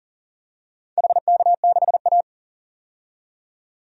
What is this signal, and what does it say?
Morse code, heard as a keyed tone, about 39 wpm, 700 Hz. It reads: HK6A
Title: DXCC-100 Call Signs - 4 Characters - 40wpm